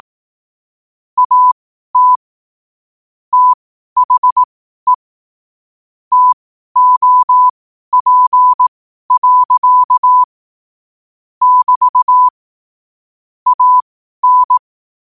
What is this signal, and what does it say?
Morse code, heard as a keyed tone, about 18 wpm, spaced as for 12 wpm, 1000 Hz. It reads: AT THE TOP. <BT> AN
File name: 12wpm